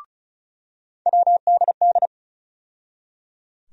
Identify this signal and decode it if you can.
Morse code, heard as a keyed tone, about 34 wpm, 700 Hz. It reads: WDD
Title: USA Call Sign Postfixes - 3 Letters 35wpm